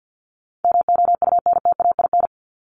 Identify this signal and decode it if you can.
Morse code, heard as a keyed tone, about 56 wpm, 700 Hz. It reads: MOUNTAIN
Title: Top 500 Words 50wpm